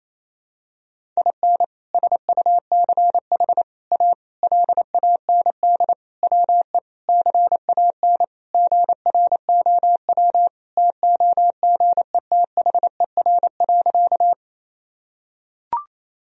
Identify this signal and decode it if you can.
Morse code, heard as a keyed tone, about 28 wpm, 700 Hz. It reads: IN SUCH A LAND WE CAN GROW TOGETHER.
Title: Sentences from Top 1000 Words - Review Part 1 - 28wpm